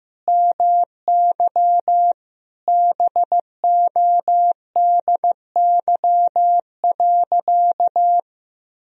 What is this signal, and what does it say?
Morse code, heard as a keyed tone, about 15 wpm, 700 Hz. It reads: MY BODY.
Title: Sentences from Top 1000 Words - Review Part 2 - 15wpm